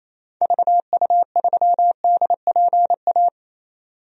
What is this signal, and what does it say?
Morse code, heard as a keyed tone, about 28 wpm, 700 Hz. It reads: VU3DPA
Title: DXCC-100 Call Signs - 6 Characters - 28wpm